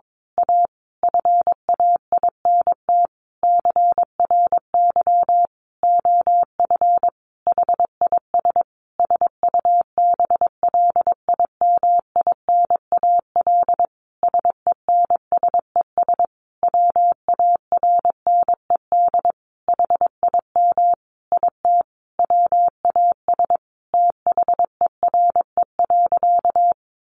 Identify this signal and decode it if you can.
Morse code, heard as a keyed tone, about 22 wpm, 700 Hz. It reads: A FAINT CRY OF HIS SUBLIMINAL SENSES WARNED HIM IT WAS THERE.
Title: A Sound Of Thunder 22wpm